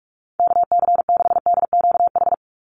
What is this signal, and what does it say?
Morse code, heard as a keyed tone, about 44 wpm, 700 Hz. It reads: KC6DQH
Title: USA Call Signs 45wpm